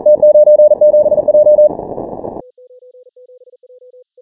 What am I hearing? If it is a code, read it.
Morse code, heard as a keyed tone, about 41 wpm, 600 Hz. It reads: T97J
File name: final / W1UJ-168-140-MorseRunner